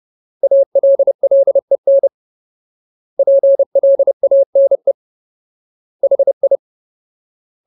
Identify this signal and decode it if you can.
Morse code, heard as a keyed tone, about 30 wpm, 550 Hz. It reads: ALLEN PLANE HI
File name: Ultimate Word List / 30WPM Ultimate